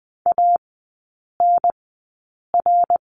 Morse code, heard as a keyed tone, about 20 wpm, spaced as for 8 wpm, 700 Hz. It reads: ANR
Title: 060 Lesson 11e - Letter h - Three Characters 20wpm